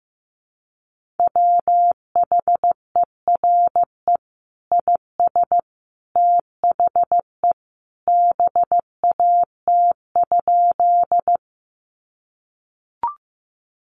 Morse code, heard as a keyed tone, about 15 wpm, 700 Hz. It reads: WHERE IS THE BAT?
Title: Sentences from Top 1500 Words 15wpm